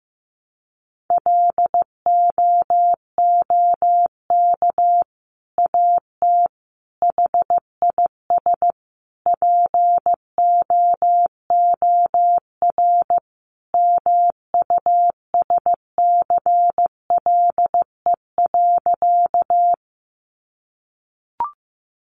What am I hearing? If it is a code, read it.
Morse code, heard as a keyed tone, about 15 wpm, 700 Hz. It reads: LOOK AT HIS POOR MUSCLE.
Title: Sentences from Top 1300 Words 15wpm